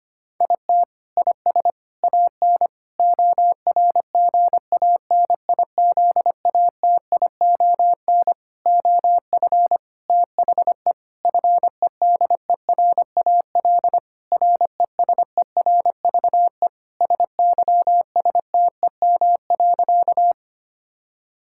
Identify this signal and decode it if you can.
Morse code, heard as a keyed tone, about 25 wpm, 700 Hz. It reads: IT IS AN ORGANIZATION OF THE FEDERAL RESERVE SYSTEM.
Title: Sentences from Top 1000 Words - Review Part 6 - 25wpm